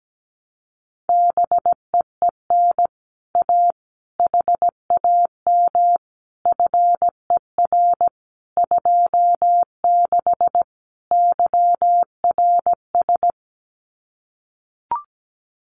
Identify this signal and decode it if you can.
Morse code, heard as a keyed tone, about 17 wpm, 700 Hz. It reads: BEEN A HAM FER 26 YRS
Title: Common QSO Phrases Encore 17wpm